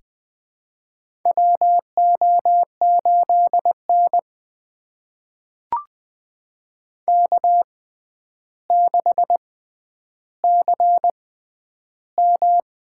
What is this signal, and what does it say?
Morse code, heard as a keyed tone, about 20 wpm, spaced as for 10 wpm, 700 Hz. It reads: WO8N K6CM
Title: USA Call Signs - 4 Characters - 6x Character Spacing - 20wpm